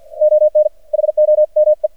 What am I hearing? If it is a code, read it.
Morse code, heard as a keyed tone, about 53 wpm, spaced as for 47 wpm, 610 Hz. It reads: ON HOME
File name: archived / CRSnet